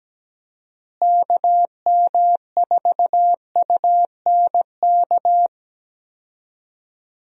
Morse code, heard as a keyed tone, about 17 wpm, 700 Hz. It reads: KM4UNK
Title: USA Call Signs - 6 Characters - Repeated 2x - 17wpm